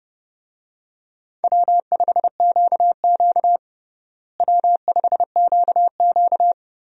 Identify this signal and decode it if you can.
Morse code, heard as a keyed tone, about 30 wpm, 700 Hz. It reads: W5QQ W5QQ
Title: USA Call Signs - 4 Characters - Repeated 3x - 30wpm